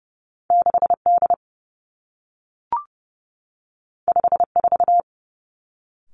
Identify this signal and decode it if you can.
Morse code, heard as a keyed tone, about 30 wpm, 700 Hz. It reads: 6 D 5 4
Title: Two Challenging Letter-Number Combinations 30wpm